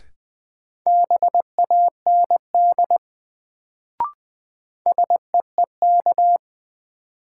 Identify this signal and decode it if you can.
Morse code, heard as a keyed tone, about 20 wpm, 700 Hz. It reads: BAND SEEK
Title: Sentences from Top 1600 Words - New Word Drill - 20wpm